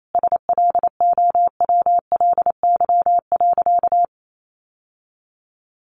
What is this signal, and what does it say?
Morse code, heard as a keyed tone, about 28 wpm, 700 Hz. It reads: SLOWLY.
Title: Sentences from Top 1100 Words 28wpm